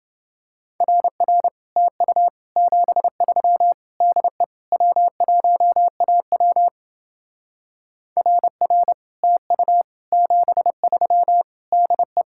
Morse code, heard as a keyed tone, about 30 wpm, 700 Hz. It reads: RR TU 73 DE W1AW RR TU 73 DE